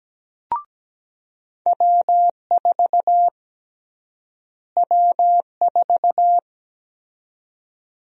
Morse code, heard as a keyed tone, about 17 wpm, 700 Hz. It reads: W4 W4
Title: USA Call Sign Prefixes - Repeated 2x - 17wpm